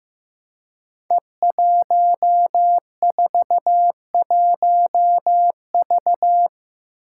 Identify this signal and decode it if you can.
Morse code, heard as a keyed tone, about 15 wpm, 700 Hz. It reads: E141V
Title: Five Challenging Letter-Number Combinations 15wpm